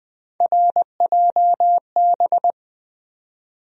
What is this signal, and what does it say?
Morse code, heard as a keyed tone, about 20 wpm, 700 Hz. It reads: RJB
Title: USA Call Sign Postfixes - 3 Letters 20wpm